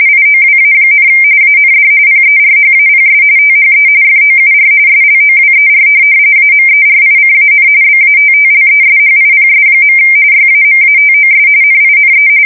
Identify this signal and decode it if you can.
RTTY, about 45.45 baud. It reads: IOWA, AND PERHAPS THE FOUR SLOWER AMERICAN BATTLESHIPS. YAMATOS NINE
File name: fsk170 / USS-New-Jersey